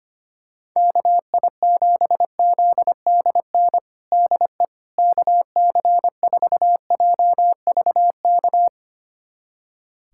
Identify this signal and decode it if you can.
Morse code, heard as a keyed tone, about 25 wpm, 700 Hz. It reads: KI7ZDN DE KC4JVK
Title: USA Callsign DE Callsign - 6 Characters - 25wpm